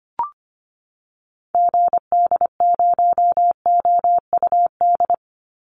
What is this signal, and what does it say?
Morse code, heard as a keyed tone, about 25 wpm, 700 Hz. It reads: GD0OUD
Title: Call Signs Repeated 2x - 25wpm